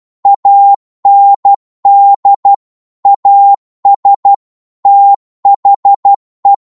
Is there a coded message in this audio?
Yes: Morse code from 0.3 to 6.5 s, heard as a keyed tone, about 12 wpm, 800 Hz. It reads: AND AS THE